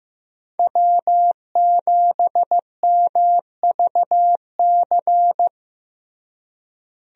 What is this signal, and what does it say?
Morse code, heard as a keyed tone, about 15 wpm, 700 Hz. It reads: W7MVC
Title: USA Call Signs - 5 Characters - Repeated 2x - 15wpm